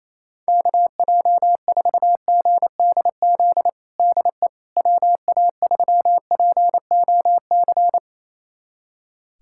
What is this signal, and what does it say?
Morse code, heard as a keyed tone, about 28 wpm, 700 Hz. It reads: KJ4GDZ DE WA3POC
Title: USA Callsign DE Callsign 28wpm